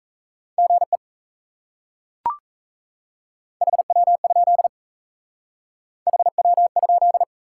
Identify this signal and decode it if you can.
Morse code, heard as a keyed tone, about 35 wpm, 700 Hz. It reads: GE HW? HW?
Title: Speed-Racing Top 90 QSO Elements 28wpm